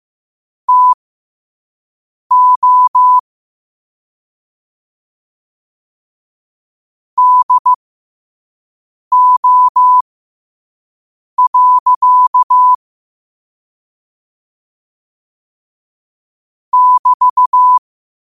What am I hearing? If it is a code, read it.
Morse code, heard as a keyed tone, about 15 wpm, spaced as for 5 wpm, 1000 Hz. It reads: TO DO. <BT>
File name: qotdfiles / 5wpm